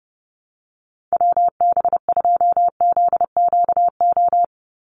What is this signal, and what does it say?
Morse code, heard as a keyed tone, about 30 wpm, 700 Hz. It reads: WB2ZQO
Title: USA Call Signs 30wpm